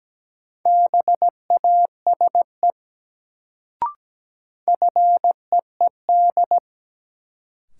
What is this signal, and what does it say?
Morse code, heard as a keyed tone, about 17 wpm, 700 Hz. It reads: BASE FEED
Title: Sentences from Top 1600 Words - New Word Drill Alt - 17wpm